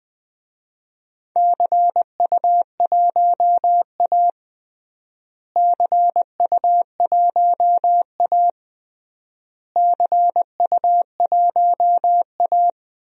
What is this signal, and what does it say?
Morse code, heard as a keyed tone, about 20 wpm, 700 Hz. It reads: CU1A CU1A CU1A
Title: DXCC-100 Call Signs - 4 Characters - Repeated 3x - 20wpm